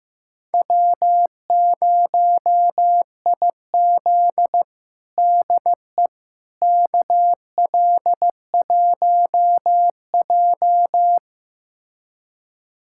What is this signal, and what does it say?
Morse code, heard as a keyed tone, about 15 wpm, 700 Hz. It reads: W0IZ DE KL1J
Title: USA Callsign DE Callsign - 4 Characters - 15wpm